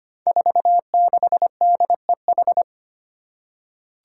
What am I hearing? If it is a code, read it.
Morse code, heard as a keyed tone, about 25 wpm, 700 Hz. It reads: 46DEH
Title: Five Challenging Letter-Number Combinations 25wpm